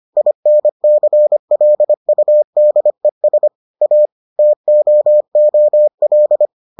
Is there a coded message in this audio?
Yes: Morse code from 0.2 to 6.5 s, heard as a keyed tone, about 25 wpm, 600 Hz. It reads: INCLUDES A TOOL